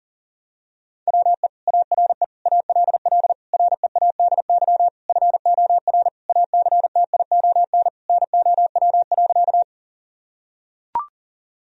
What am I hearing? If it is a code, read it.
Morse code, heard as a keyed tone, about 37 wpm, 700 Hz. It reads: WE ARE ALL READY FOR ACTION NOW.
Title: Sentences from Top 1000 Words - Review Part 4 - 40wpm